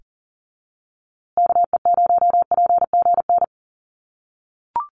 Morse code, heard as a keyed tone, about 40 wpm, 700 Hz. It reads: KE0PGN
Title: USA Call Signs Repeated 3x 40wpm